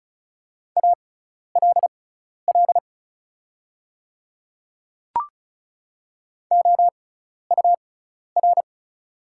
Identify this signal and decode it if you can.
Morse code, heard as a keyed tone, about 34 wpm, spaced as for 12 wpm, 700 Hz. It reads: ALL OUR
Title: 6x Character Spacing - Top 200 Words - 35wpm